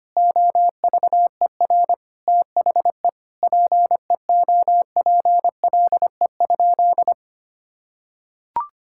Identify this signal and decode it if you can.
Morse code, heard as a keyed tone, about 25 wpm, 700 Hz. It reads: OVER THE PEOPLE?
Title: Sentences from Top 1000 Words - Review Part 5 - 25wpm